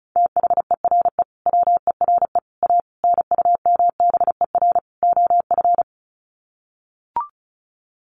Morse code, heard as a keyed tone, about 34 wpm, 700 Hz. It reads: THERE WERE A NUMBER OF
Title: Sets of 5 Words Encore2 35wpm